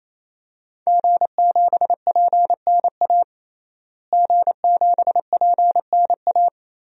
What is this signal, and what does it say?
Morse code, heard as a keyed tone, about 28 wpm, 700 Hz. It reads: G7PNA G7PNA
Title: DXCC-100 Call Signs - 5 Characters - Repeated 3x - 28wpm